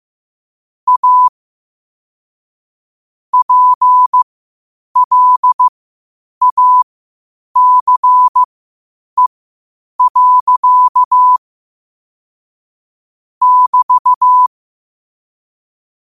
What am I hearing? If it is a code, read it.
Morse code, heard as a keyed tone, about 15 wpm, spaced as for 8 wpm, 1000 Hz. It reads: A PLACE. <BT>
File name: qotdfiles / 8wpm